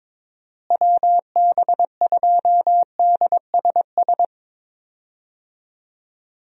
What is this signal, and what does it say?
Morse code, heard as a keyed tone, about 21 wpm, 700 Hz. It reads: WB2DSS
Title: USA Call Signs - 6 Characters - Repeated 2x - 22wpm